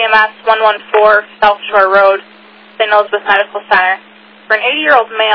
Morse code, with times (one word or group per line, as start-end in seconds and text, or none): none